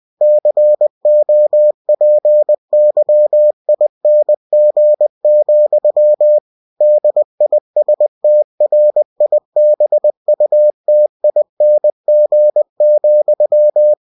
Morse code, heard as a keyed tone, about 20 wpm, 600 Hz. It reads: COPYING, DISTRIBUTING,
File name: cw-books / winnie-the-pooh / winnie-20wpm-page-0114